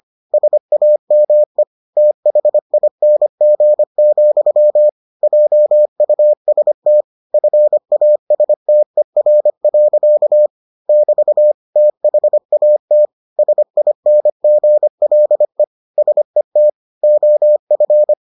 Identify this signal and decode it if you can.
Morse code, heard as a keyed tone, about 25 wpm, 600 Hz. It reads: SAME THING, JUST FASTER. <BT> THAT SINGLE SET OF